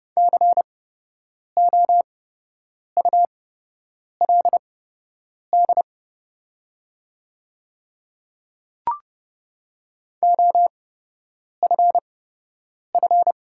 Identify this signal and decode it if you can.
Morse code, heard as a keyed tone, about 30 wpm, spaced as for 8 wpm, 700 Hz. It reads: COULD OFF
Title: 8x Character Spacing - Top 300 Words - 30wpm